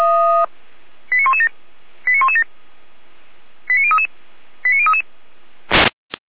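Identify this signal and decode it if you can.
Five-tone selective call (ZVEI), heard as quick runs of five tones, tones of about 70 ms. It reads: 89117 89117 89033 89033
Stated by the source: ZVEI five-tone selective call 89080, received via SDR.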